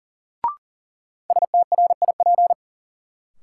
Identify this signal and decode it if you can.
Morse code, heard as a keyed tone, about 40 wpm, 700 Hz. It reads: STRIP
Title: Sentences from Top 1600 Words - New Word Drill - 40wpm